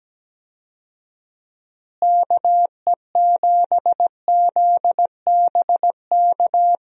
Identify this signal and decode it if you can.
Morse code, heard as a keyed tone, about 17 wpm, 700 Hz. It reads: KE7ZBK
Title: USA Call Signs - 6 Characters - Repeated 3x - 17wpm